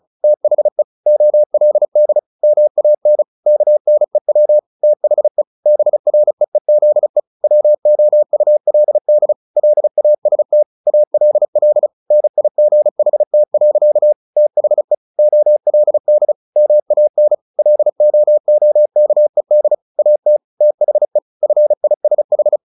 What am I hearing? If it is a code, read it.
Morse code, heard as a keyed tone, about 34 wpm, 600 Hz. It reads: THE OLD MAN KNEW THE BREEZE WOULD LAST ALL NIGHT. THE OLD MAN LOOKED AT THE FISH
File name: the old man and the sea 2 35wpm